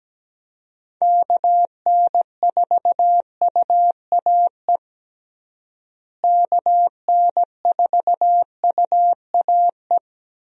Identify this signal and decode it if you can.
Morse code, heard as a keyed tone, about 17 wpm, 700 Hz. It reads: KN4UAE KN4UAE
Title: USA Call Signs - 6 Characters - Repeated 3x - 17wpm